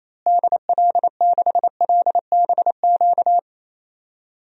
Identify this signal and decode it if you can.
Morse code, heard as a keyed tone, about 27 wpm, 700 Hz. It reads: DL6LBQ
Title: DXCC-100 Call Signs - 6 Characters - Repeated 3x - 28wpm